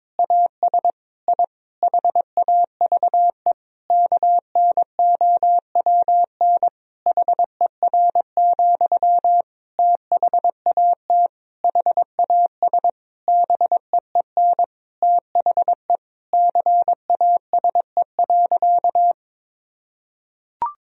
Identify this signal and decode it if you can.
Morse code, heard as a keyed tone, about 22 wpm, 700 Hz. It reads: AS I HAVE KNOWN HER, THAT HAS BEEN THE CASE.
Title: Sentences from Top 1000 Words - Review Part 3 - 22wpm